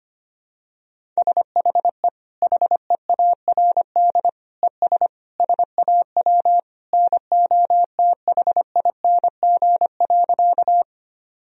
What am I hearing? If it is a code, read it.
Morse code, heard as a keyed tone, about 25 wpm, 700 Hz. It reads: SHE HEARD ES SAW NOTHING.
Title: Sentences from Top 500 Words and Abbreviations 25wpm